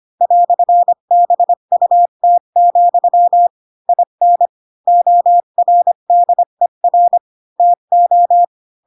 Morse code, heard as a keyed tone, about 25 wpm, 700 Hz. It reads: "BUT, IN ORDER TO
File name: Pinocchio / Pinocchio ch8 20 25 ik1hge